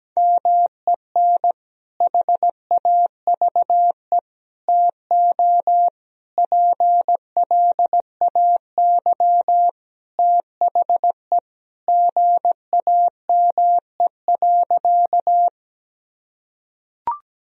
Morse code, heard as a keyed tone, about 17 wpm, 700 Hz. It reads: MEN HAVE TO PLAY THE GAME.
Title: Sentences from Top 1000 Words - Review Part 3 - 17wpm